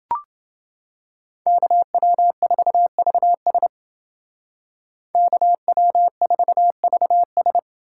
Morse code, heard as a keyed tone, about 28 wpm, 700 Hz. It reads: KW4VS KW4VS